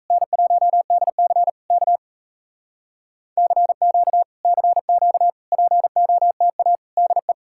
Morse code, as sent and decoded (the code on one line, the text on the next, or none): -. .---- -.. -.-. / -.- / -.-. --.- / -.-. --.- / .--. --- - .- / -.. .
N1DC K CQ CQ POTA DE